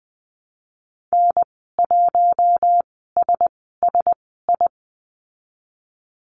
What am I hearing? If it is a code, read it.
Morse code, heard as a keyed tone, about 20 wpm, spaced as for 15 wpm, 700 Hz. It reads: N1SSI
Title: 042 Lesson 8g - Number 1 - Call Signs 20wpm